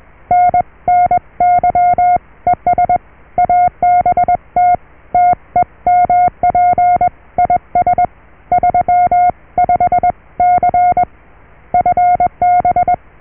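Morse code, heard as a keyed tone, about 21 wpm, 720 Hz. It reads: NNY ES ABT TEMP IS 35C FB
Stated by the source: Morse QSO on 14 MHz with a German station, received at RST 579.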